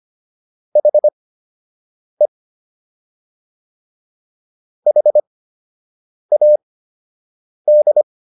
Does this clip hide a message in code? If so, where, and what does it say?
Yes: Morse code from 0.8 to 8.0 s, heard as a keyed tone, about 25 wpm, spaced as for 7 wpm, 600 Hz. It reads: HE HAD